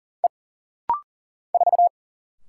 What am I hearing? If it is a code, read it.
Morse code, heard as a keyed tone, about 39 wpm, 700 Hz. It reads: E 4
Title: Single Letter-Number - Rapid-Fire 40wpm